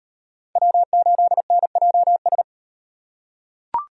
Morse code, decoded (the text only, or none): W8NJS